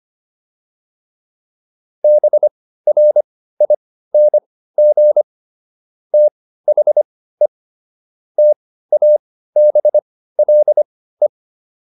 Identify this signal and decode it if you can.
Morse code, heard as a keyed tone, about 25 wpm, spaced as for 15 wpm, 600 Hz. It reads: BRING THE TABLE